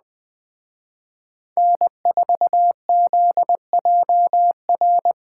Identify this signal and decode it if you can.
Morse code, heard as a keyed tone, about 20 wpm, 700 Hz. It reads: N4ZJR
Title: Speed Racing - USA 1x3 Call Signs 17wpm